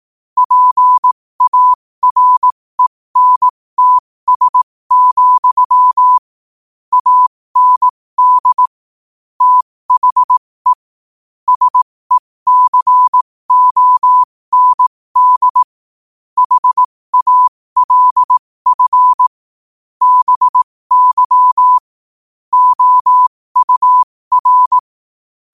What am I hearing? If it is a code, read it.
Morse code, heard as a keyed tone, about 18 wpm, spaced as for 15 wpm, 1000 Hz. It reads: PARENTS, AND THE SECOND HALF BY OUR
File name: 15wpm